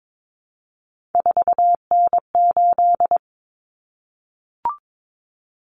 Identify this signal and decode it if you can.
Morse code, heard as a keyed tone, about 22 wpm, 700 Hz. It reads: 4N8
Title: Common Cut Numbers - 3-Digit 22wpm